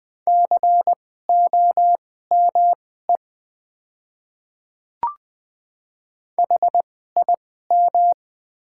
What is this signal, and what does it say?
Morse code, heard as a keyed tone, about 20 wpm, spaced as for 14 wpm, 690 Hz. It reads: COME HIM
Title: 2x Character Spacing - Top 100 Words - 20wpm